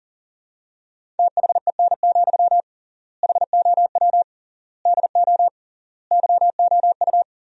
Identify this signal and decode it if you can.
Morse code, heard as a keyed tone, about 40 wpm, 700 Hz. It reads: THEN, HOW DO YOU
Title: 3x Word-Spacing - Sentences from Top 100 Words - 40wpm